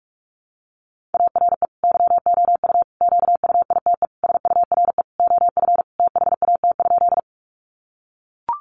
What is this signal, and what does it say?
Morse code, heard as a keyed tone, about 44 wpm, 700 Hz. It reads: ARE YOU QUITE SURE OF THAT?
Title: Sentences from Top 700 Words 45wpm